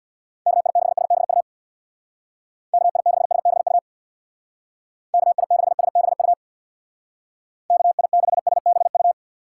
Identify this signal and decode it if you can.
Morse code, heard as a keyed tone, about 64 wpm, 700 Hz. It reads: KI6SBU KI6SBU KI6SBU KI6SBU